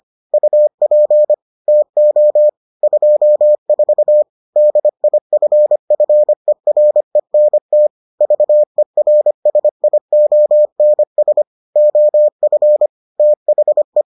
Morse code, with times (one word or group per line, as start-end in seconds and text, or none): 0.3-1.3 UP
1.7-2.5 TO
2.8-4.2 24
4.6-7.9 DIFFERENT
8.2-11.4 VERSIONS
11.8-12.9 OF
13.2-14.0 THE